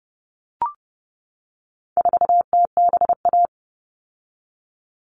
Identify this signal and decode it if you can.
Morse code, heard as a keyed tone, about 30 wpm, 700 Hz. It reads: 4TBA
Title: All Cut Numbers - 4-Digit 30wpm